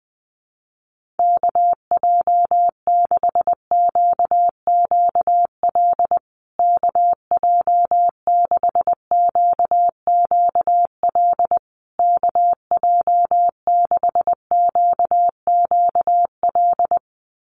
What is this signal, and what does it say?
Morse code, heard as a keyed tone, about 20 wpm, 700 Hz. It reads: KJ6QQL KJ6QQL KJ6QQL
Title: USA Call Signs Repeated 3x 20wpm